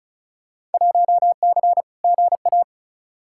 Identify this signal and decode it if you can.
Morse code, heard as a keyed tone, about 36 wpm, 700 Hz. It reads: 1C GA
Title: Field Day Exchanges Encore - 35wpm